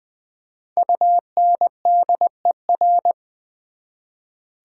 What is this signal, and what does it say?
Morse code, heard as a keyed tone, about 24 wpm, 700 Hz. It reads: UNDER
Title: Speed-Racing Prefixes and Suffixes 20wpm